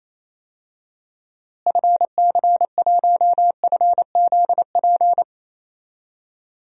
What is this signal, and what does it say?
Morse code, heard as a keyed tone, about 28 wpm, 700 Hz. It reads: FC1FZP
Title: DXCC-100 Call Signs - 6 Characters - Repeated 3x - 28wpm